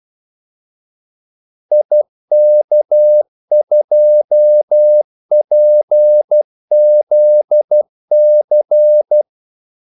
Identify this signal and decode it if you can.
Morse code, heard as a keyed tone, about 12 wpm, 600 Hz. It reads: IK2PZC
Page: Kaldesignaler 12 wpm